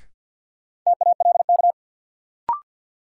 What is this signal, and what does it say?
Morse code, heard as a keyed tone, about 52 wpm, 700 Hz. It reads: TALK
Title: Sentences from Top 400 Words - New Word Drill - 50wpm